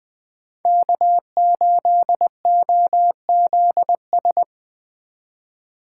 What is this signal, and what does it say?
Morse code, heard as a keyed tone, about 21 wpm, 700 Hz. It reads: K8OZS
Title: Speed Racing - USA Call Signs 17wpm